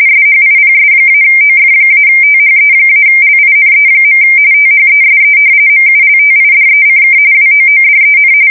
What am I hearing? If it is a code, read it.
RTTY, about 45.45 baud. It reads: SHIPS-OF-THE-LINE. THESE MULTIDECK WOODEN
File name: fsk170 / battleshipnj